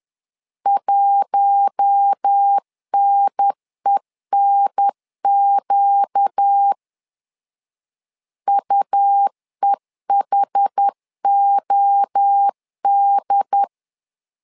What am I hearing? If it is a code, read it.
Morse code, heard as a keyed tone, about 10 wpm, 800 Hz. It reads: 1NENQ UEHOD
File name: LEZIONI / 4 Q1S 10wpm 2133